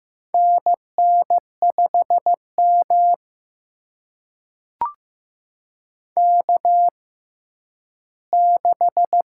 Morse code, heard as a keyed tone, about 15 wpm, spaced as for 10 wpm, 700 Hz. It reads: NN5M K6
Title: USA Call Signs - 4 Characters - 6x Character Spacing - 15wpm